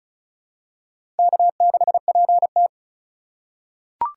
Morse code, heard as a keyed tone, about 36 wpm, 700 Hz. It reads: K6PT
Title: USA Call Signs - 2x Character Spacing - 35wpm